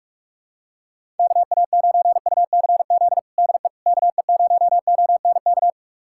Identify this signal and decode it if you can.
Morse code, heard as a keyed tone, about 46 wpm, 700 Hz. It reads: KA9UCZ DE KE0ONK
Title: USA Callsign DE Callsign - 6 Characters - 45wpm